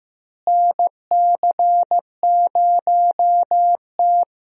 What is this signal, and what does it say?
Morse code, heard as a keyed tone, about 15 wpm, 700 Hz. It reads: NC0T
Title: USA Call Signs - 4 Characters - Repeated 3x - 15wpm